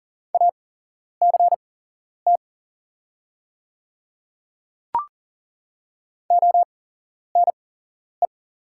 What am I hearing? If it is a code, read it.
Morse code, heard as a keyed tone, about 37 wpm, spaced as for 11 wpm, 700 Hz. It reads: ACT ONE
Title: 8x Character Spacing - Top 200 Words - 40wpm